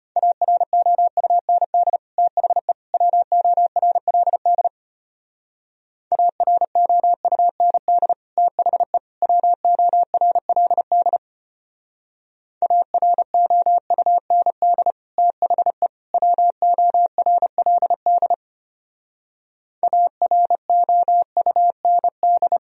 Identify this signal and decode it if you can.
Morse code, heard as a keyed tone, about 31 wpm, 700 Hz. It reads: AROUND THE WORLD AROUND THE WORLD AROUND THE WORLD AROUND
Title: Speed-Racing - Sets of 3 Words 25wpm